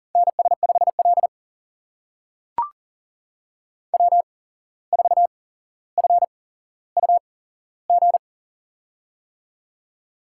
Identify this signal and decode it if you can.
Morse code, heard as a keyed tone, about 41 wpm, spaced as for 16 wpm, 700 Hz. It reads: NS5L W4FUG
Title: USA Call Signs - 8x Character Spacing - 40wpm